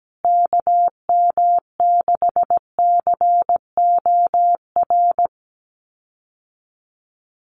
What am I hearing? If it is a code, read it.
Morse code, heard as a keyed tone, about 17 wpm, 700 Hz. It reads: KM6COR
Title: USA Call Signs Repeated 3x 17wpm